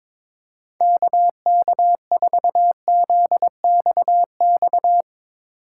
Morse code, heard as a keyed tone, about 22 wpm, 700 Hz. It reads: KK4ZXX
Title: USA Call Signs - 4x Character Spacing - 22wpm